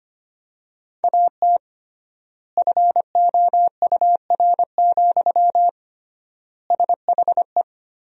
Morse code, heard as a keyed tone, about 25 wpm, 700 Hz. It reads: AT FOUR, SHE
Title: 3x Word-Spacing - Sentences from Top 300 Words - 25wpm